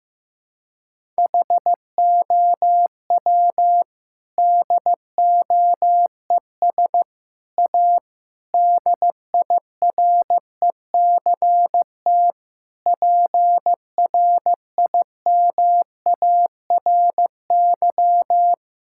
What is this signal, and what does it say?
Morse code, heard as a keyed tone, about 15 wpm, 700 Hz. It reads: HOW DOES A DIRECT PRIMARY
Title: Sentences from Top 1000 Words - Review Part 5 - 15wpm